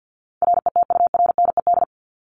Morse code, heard as a keyed tone, about 50 wpm, 700 Hz. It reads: RETURNED
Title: ED Suffix - Top 100 Words with Base Words 50wpm